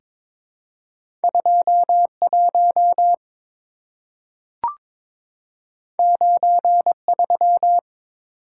Morse code, heard as a keyed tone, about 22 wpm, 700 Hz. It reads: 21 93
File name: practice / 22wpm / Two Letter-Number Combinations - 22wpm